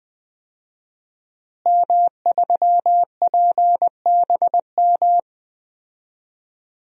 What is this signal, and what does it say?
Morse code, heard as a keyed tone, about 20 wpm, 700 Hz. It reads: M3PBM
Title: DXCC-100 Call Signs - 5 Characters - Repeated 2x - 20wpm